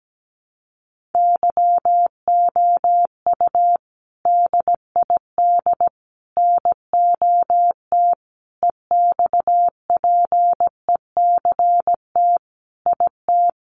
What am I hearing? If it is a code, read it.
Morse code, heard as a keyed tone, about 17 wpm, 700 Hz. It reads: YOU DID NOT EXPECT IT
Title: Sentences from Top 1100 Words 17wpm